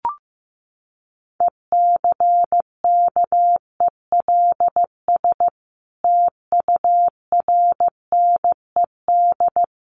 Morse code, heard as a keyed tone, about 15 wpm, 700 Hz. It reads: ECKELS TURNED
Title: A Sound Of Thunder 15wpm